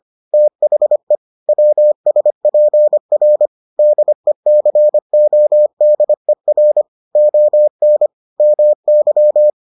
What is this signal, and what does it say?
Morse code, heard as a keyed tone, about 25 wpm, 600 Hz. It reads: THE WSPR DECODER ON MY